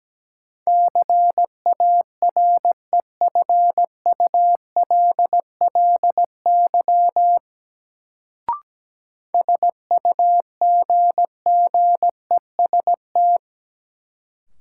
Morse code, heard as a keyed tone, about 17 wpm, 700 Hz. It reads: CAREFULLY SUGGEST
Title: Sentences from Top 1300 Words - New Word Drill - 17wpm